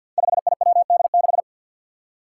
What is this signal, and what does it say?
Morse code, heard as a keyed tone, about 47 wpm, 700 Hz. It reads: 5IWD6
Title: Five Challenging Letter-Number Combinations 50wpm